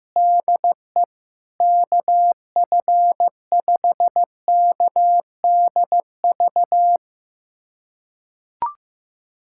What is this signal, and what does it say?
Morse code, heard as a keyed tone, about 15 wpm, 700 Hz. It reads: DE KF5KDV
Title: USA Callsign DE Callsign 15wpm